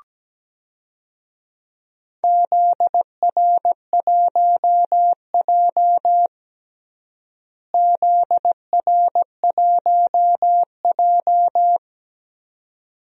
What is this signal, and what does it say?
Morse code, heard as a keyed tone, about 17 wpm, 700 Hz. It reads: ZR1J ZR1J
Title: DXCC-100 Call Signs - 4 Characters - Repeated 2x - 17wpm